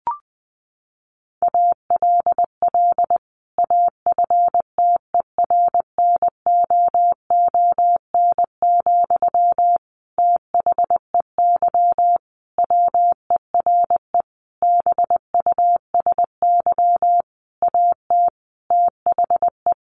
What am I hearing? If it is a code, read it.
Morse code, heard as a keyed tone, about 20 wpm, 700 Hz. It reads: ALL AFTERNOON, THEY WERE BUSY AT THE
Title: Sentences from Top 1500 Words 20wpm